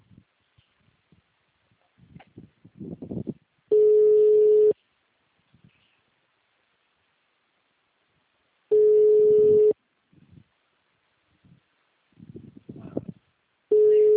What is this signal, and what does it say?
Ringback tone (Europe, 425 Hz).